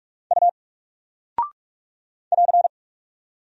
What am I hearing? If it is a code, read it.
Morse code, heard as a keyed tone, about 44 wpm, 700 Hz. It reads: U <AR>
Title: Single Character (Letter:Number:Punctuation:Prosign) 45wpm